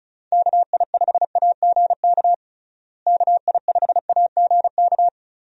Morse code, heard as a keyed tone, about 36 wpm, 700 Hz. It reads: KI5AGK KI5AGK
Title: USA Call Signs - 6 Characters - Repeated 2x - 35wpm